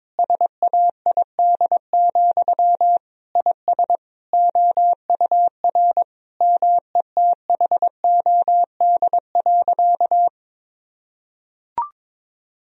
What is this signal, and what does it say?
Morse code, heard as a keyed tone, about 21 wpm, 700 Hz. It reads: SAID, IS OUR METHOD.
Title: Sentences from Top 1000 Words - Review Part 3 - 22wpm